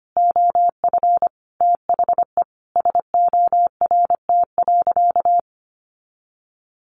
Morse code, heard as a keyed tone, about 25 wpm, 700 Hz. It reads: OF THE SORT.
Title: Sentences from Top 1300 Words 25wpm